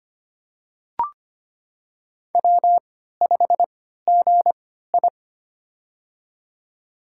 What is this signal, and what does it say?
Morse code, heard as a keyed tone, about 25 wpm, spaced as for 14 wpm, 700 Hz. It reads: W5GI
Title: USA Call Signs - 4 Characters - 3x Character Spacing - 25wpm